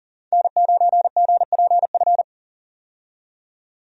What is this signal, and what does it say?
Morse code, heard as a keyed tone, about 40 wpm, 700 Hz. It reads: N9GPF
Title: USA Call Signs - 5 Characters - 40wpm